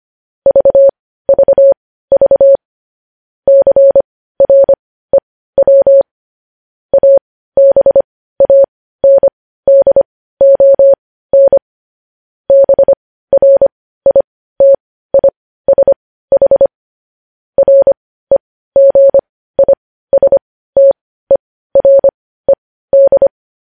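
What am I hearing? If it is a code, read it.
Morse code, heard as a keyed tone, about 25 wpm, spaced as for 15 wpm, 550 Hz. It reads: VVV CREW ABANDON BRITISH REGISTERED
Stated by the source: Morse code news headlines sent at 15 WPM.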